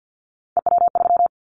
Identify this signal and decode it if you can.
Morse code, heard as a keyed tone, about 50 wpm, 700 Hz. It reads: EW3
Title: Callsign Prefixes for Top 100 DXCC - 50wpm